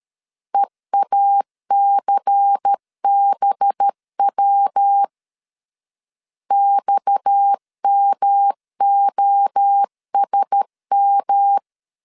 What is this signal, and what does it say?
Morse code, heard as a keyed tone, about 13 wpm, 800 Hz. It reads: EACBW XMOSM
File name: CW / DET / 1 6 1500C S A 12wpm 3440